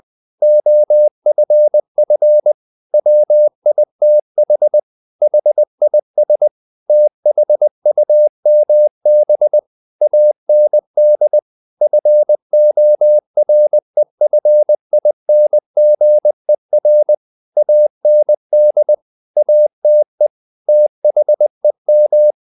Morse code, heard as a keyed tone, about 20 wpm, 600 Hz. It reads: OFF WITH HIS THUMB AND FOREFINGER AND ATE THEM
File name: the old man and the sea part 2 20wpm